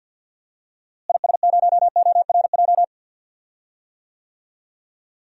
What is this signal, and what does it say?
Morse code, heard as a keyed tone, about 50 wpm, 700 Hz. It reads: IS0ORJ
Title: DXCC-100 Call Signs - 6 Characters - Repeated 2x - 50wpm